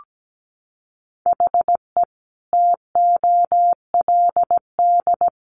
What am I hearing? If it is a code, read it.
Morse code, heard as a keyed tone, about 17 wpm, 700 Hz. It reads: HE TOLD
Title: Sentences from Top 1300 Words 17wpm